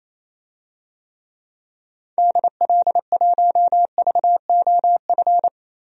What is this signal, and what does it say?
Morse code, heard as a keyed tone, about 28 wpm, 700 Hz. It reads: DL1VOF
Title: DXCC-100 Call Signs - 6 Characters - 28wpm